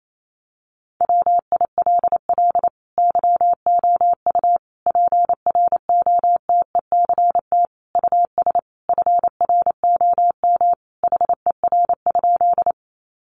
Morse code, heard as a keyed tone, about 28 wpm, 700 Hz. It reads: WILL YOU PROTECT US FROM HER?
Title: Sentences from Top 900 Words 28wpm